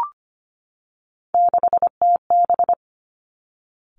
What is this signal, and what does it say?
Morse code, heard as a keyed tone, about 25 wpm, 700 Hz. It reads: 6TB
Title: Three Challenging Letter-Number Combinations 25wpm